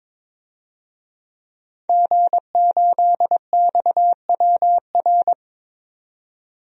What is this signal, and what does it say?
Morse code, heard as a keyed tone, about 22 wpm, 700 Hz. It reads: G8XWR
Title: DXCC-100 Call Signs - 5 Characters - 22wpm